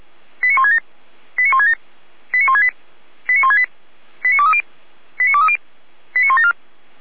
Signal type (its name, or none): five-tone selective call